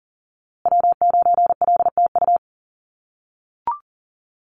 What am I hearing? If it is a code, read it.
Morse code, heard as a keyed tone, about 40 wpm, 700 Hz. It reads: W9LTU
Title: USA Call Signs Repeated 3x 40wpm